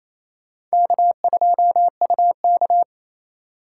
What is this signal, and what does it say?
Morse code, heard as a keyed tone, about 34 wpm, 700 Hz. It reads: K2UK
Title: Speed Racing - USA 1x2 Call Signs 28wpm